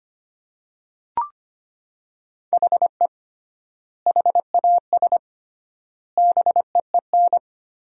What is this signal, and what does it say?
Morse code, heard as a keyed tone, about 25 wpm, 700 Hz. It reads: HE HAS BEEN
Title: 3x Word-Spacing - Sentences from Top 300 Words - 25wpm